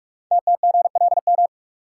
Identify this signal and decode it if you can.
Morse code, heard as a keyed tone, about 44 wpm, 700 Hz. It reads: TTGLM
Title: Code Groups - 5 Letters - 45wpm